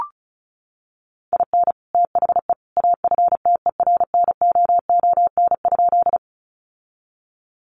Morse code, heard as a keyed tone, about 34 wpm, 700 Hz. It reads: IN THE AFTERNOON?
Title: Sentences from Top 1500 Words 35wpm